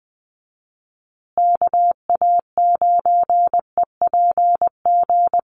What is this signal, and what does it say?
Morse code, heard as a keyed tone, about 20 wpm, 700 Hz. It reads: KA9EPG
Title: USA Call Signs Repeated 3x 20wpm